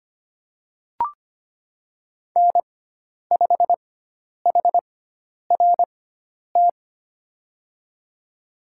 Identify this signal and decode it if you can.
Morse code, heard as a keyed tone, about 25 wpm, spaced as for 10 wpm, 700 Hz. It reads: N5HRT
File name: Lesson 16g - Letter c - Call Signs 25wpm